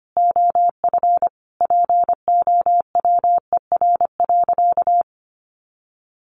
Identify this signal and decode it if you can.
Morse code, heard as a keyed tone, about 25 wpm, 700 Hz. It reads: OF POWER.
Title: Sentences from Top 1300 Words 25wpm